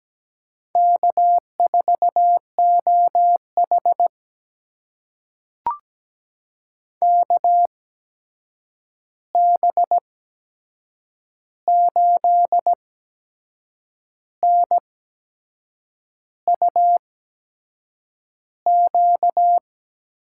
Morse code, heard as a keyed tone, about 17 wpm, spaced as for 6 wpm, 700 Hz. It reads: K4OH KB8NUQ
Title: USA Call Signs - 8x Character Spacing - 17wpm